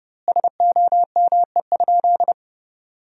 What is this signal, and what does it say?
Morse code, heard as a keyed tone, about 30 wpm, 700 Hz. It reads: SOME?
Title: 2x Word-Spacing - Sentences from Top 100 Words - 30wpm